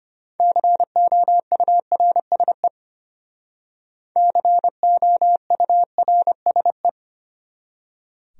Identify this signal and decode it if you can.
Morse code, heard as a keyed tone, about 32 wpm, 700 Hz. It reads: COURSE COURSE